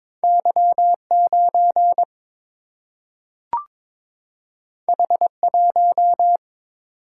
Morse code, heard as a keyed tone, about 22 wpm, 700 Hz. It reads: Y 9 H 1
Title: Single-Letter Single-Number 22wpm